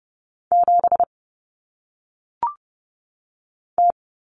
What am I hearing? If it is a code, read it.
Morse code, heard as a keyed tone, about 30 wpm, 700 Hz. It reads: 7 T
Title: Common Cut Numbers - 1-Digit 30wpm